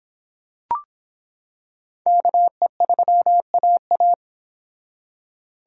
Morse code, heard as a keyed tone, about 25 wpm, 700 Hz. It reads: KE3AA